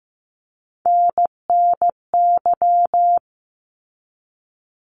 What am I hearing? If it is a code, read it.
Morse code, heard as a keyed tone, about 15 wpm, 700 Hz. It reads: NNY
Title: ARRL Sections 15wpm